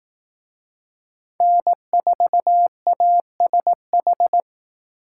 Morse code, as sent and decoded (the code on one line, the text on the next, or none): -. ....- .- ... ....
N4ASH